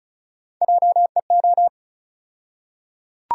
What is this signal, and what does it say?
Morse code, heard as a keyed tone, about 34 wpm, 700 Hz. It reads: JEO
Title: USA Call Sign Postfixes - 3 Letters 35wpm